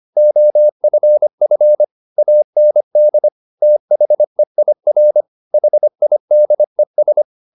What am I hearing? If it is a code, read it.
Morse code, heard as a keyed tone, about 25 wpm, 600 Hz. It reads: OFF AND THEIR HIDES